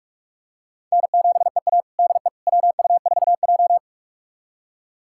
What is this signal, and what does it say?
Morse code, heard as a keyed tone, about 46 wpm, 700 Hz. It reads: N7EA DE WU4J
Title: USA Callsign DE Callsign - 4 Characters - 45wpm